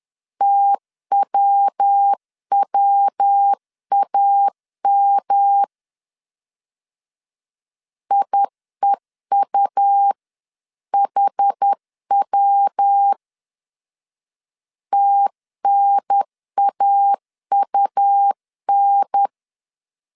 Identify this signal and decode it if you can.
Morse code, heard as a keyed tone, about 10 wpm, 800 Hz. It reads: TWWAM IEUHW TNAUN
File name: 10wpm